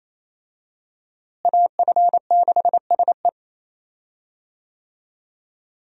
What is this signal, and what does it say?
Morse code, heard as a keyed tone, about 28 wpm, 700 Hz. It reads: AF6SE
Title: USA Call Signs - 5 Characters - Repeated 3x - 28wpm